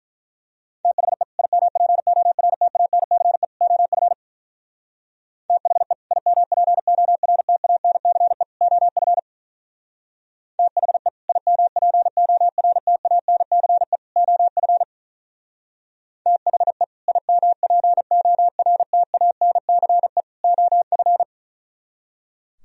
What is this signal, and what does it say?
Morse code, heard as a keyed tone, about 44 wpm, 700 Hz. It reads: THE IMPORTANCE OF THE IMPORTANCE OF THE IMPORTANCE OF THE IMPORTANCE OF
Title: Speed-Racing - Sets of 3 Words 35wpm